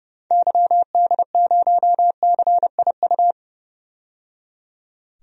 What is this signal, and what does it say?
Morse code, heard as a keyed tone, about 30 wpm, 700 Hz. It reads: YD0CIU
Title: DXCC-100 Call Signs - 6 Characters - 30wpm